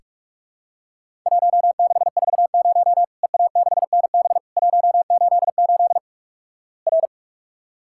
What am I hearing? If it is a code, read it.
Morse code, heard as a keyed tone, about 46 wpm, 700 Hz. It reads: 1640 EA6NB 188 R
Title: Worked All Europe DX Contest - QTC Exchange - 45wpm